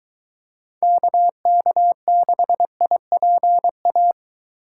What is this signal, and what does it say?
Morse code, heard as a keyed tone, about 21 wpm, 700 Hz. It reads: KK6IPA